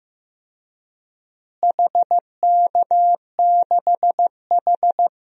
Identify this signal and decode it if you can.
Morse code, heard as a keyed tone, about 16 wpm, 700 Hz. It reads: HK6H
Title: DXCC-100 Call Signs - 4 Characters - Repeated 2x - 15wpm